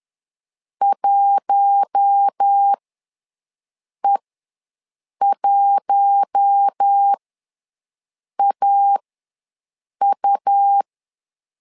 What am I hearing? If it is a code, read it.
Morse code, heard as a keyed tone, about 10 wpm, spaced as for 5 wpm, 800 Hz. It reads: 1E1AU
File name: LEZIONI / 4 Q1 ws 10wpm 2538